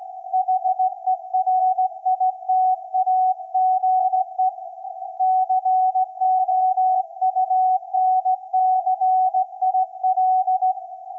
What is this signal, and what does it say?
Morse code, heard as a keyed tone, about 17 wpm, 750 Hz. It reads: HERITAGE COUNCIL
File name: iz7slz / VLF / SAQ 24-12-2024